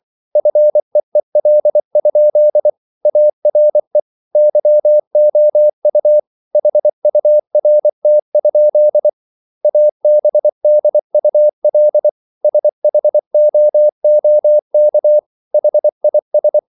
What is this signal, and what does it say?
Morse code, heard as a keyed tone, about 24 wpm, 600 Hz. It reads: FEEL? ARE YOU HURT? ABDUL SHOOK HIS